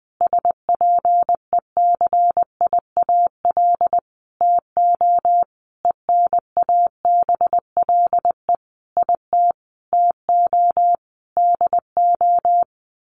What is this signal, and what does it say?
Morse code, heard as a keyed tone, about 20 wpm, 700 Hz. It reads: SPECIAL TO ENABLE IT TO DO
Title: Sentences from Top 1500 Words 20wpm